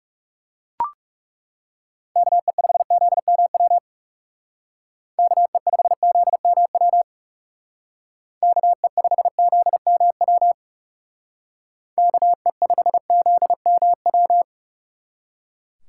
Morse code, heard as a keyed tone, about 38 wpm, 700 Hz. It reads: KE5ZMW KE5ZMW KE5ZMW KE5ZMW